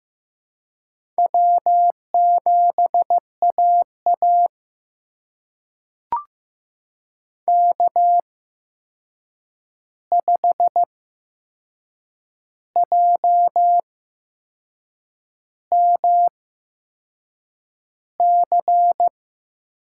Morse code, heard as a keyed tone, about 15 wpm, spaced as for 6 wpm, 700 Hz. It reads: W7AA K5JMC
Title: USA Call Signs - 8x Character Spacing - 15wpm